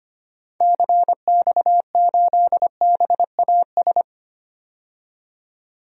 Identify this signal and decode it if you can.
Morse code, heard as a keyed tone, about 25 wpm, 700 Hz. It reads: CX8BAS
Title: DXCC-100 Call Signs - 6 Characters - 25wpm